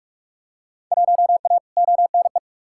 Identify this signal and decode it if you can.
Morse code, heard as a keyed tone, about 44 wpm, 700 Hz. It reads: 1A ONE
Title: Field Day Exchanges - 45wpm